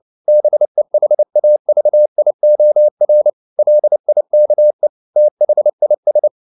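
Morse code, heard as a keyed tone, about 29 wpm, 600 Hz. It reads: BEHAVIOR LIKE THIS
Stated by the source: Morse code at 29 wpm, sending Ken Follett's novel Never in English, from CW med Gnister.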